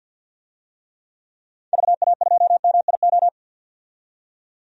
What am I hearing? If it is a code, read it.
Morse code, heard as a keyed tone, about 52 wpm, 700 Hz. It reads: VA2MIO
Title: DXCC-100 Call Signs - 6 Characters - 50wpm